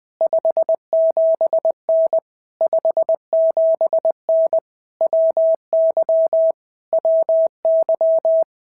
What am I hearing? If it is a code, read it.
Morse code, heard as a keyed tone, about 20 wpm, 650 Hz. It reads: 57N 57N WY WY
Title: POTA QSO Practice - 20wpm